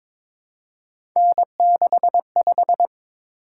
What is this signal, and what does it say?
Morse code, heard as a keyed tone, about 22 wpm, 700 Hz. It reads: N65
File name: practice / 22wpm / Three Letter-Number Combinations - 22wpm